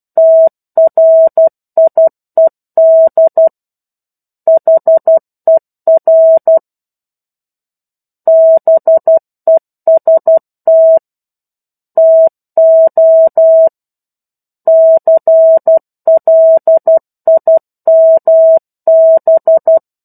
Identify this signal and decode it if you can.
Morse code, heard as a keyed tone, about 12 wpm, 650 Hz. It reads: TRIED HER BEST TO CLIMB
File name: 01-Chapter-01 12 WPM